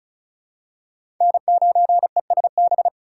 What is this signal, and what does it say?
Morse code, heard as a keyed tone, about 36 wpm, 700 Hz. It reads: N9ESB
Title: USA Call Signs - 5 Characters - 8x Character Spacing - 35wpm